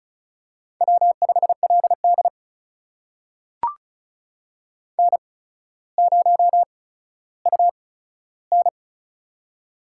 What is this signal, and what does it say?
Morse code, heard as a keyed tone, about 36 wpm, spaced as for 15 wpm, 700 Hz. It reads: W5LD N0UN
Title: USA Call Signs - 4 Characters - 8x Character Spacing - 35wpm